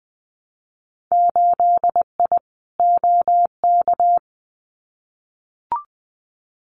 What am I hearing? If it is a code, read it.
Morse code, heard as a keyed tone, about 20 wpm, 700 Hz. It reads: 8I OK
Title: Winter Field Day Exchanges Encore 20wpm